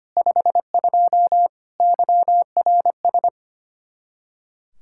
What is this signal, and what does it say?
Morse code, heard as a keyed tone, about 32 wpm, 700 Hz. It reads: 52 YRS